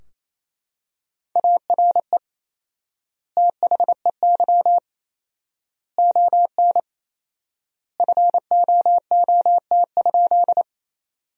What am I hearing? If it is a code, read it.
Morse code, heard as a keyed tone, about 28 wpm, 700 Hz. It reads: ARE THEY ON FOOT?
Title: 4x Word-Spacing - Sentences from Top 500 Words - 28wpm